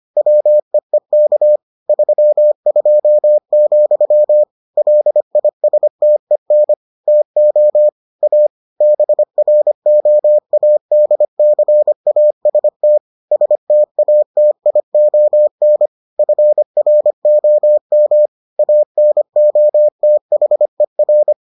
Morse code, heard as a keyed tone, about 25 wpm, 600 Hz. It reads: WEEK 32, LISTEN TO A BROADCAST STATION FROM ANOTHER